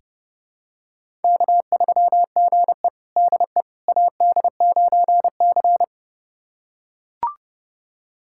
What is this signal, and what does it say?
Morse code, heard as a keyed tone, about 32 wpm, 700 Hz. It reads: K3GE DE AD9C
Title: USA Callsign DE Callsign - 4 Characters - 30wpm